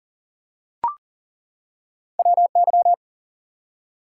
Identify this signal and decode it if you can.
Morse code, heard as a keyed tone, about 39 wpm, 700 Hz. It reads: WY
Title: US States and Canadian Provinces 40wpm